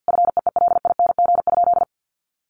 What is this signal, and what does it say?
Morse code, heard as a keyed tone, about 49 wpm, 700 Hz. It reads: FEELING?
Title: Sentences from Top 1200 Words 50wpm